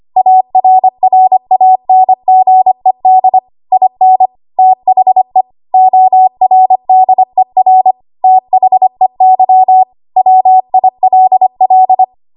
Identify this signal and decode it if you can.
Morse code, heard as a keyed tone, about 25 wpm, 750 Hz. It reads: ARRANGED IN THE ORDER THEY WILL
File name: Morse / 25WPM